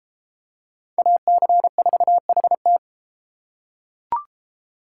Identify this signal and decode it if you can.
Morse code, heard as a keyed tone, about 30 wpm, 700 Hz. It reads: AC4HT